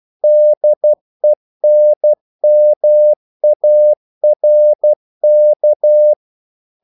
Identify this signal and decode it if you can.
Morse code, heard as a keyed tone, about 12 wpm, 600 Hz. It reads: DENMARK
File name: CW / Practice / NationalCapitals12